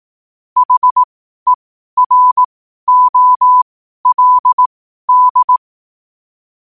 Morse code, heard as a keyed tone, about 18 wpm, spaced as for 12 wpm, 1000 Hz. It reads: HEROLD
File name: 12wpm